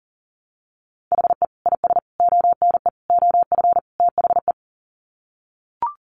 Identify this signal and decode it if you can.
Morse code, heard as a keyed tone, about 40 wpm, 700 Hz. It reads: HE IS ONE OF THE
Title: Sets of 5 Words Encore 40wpm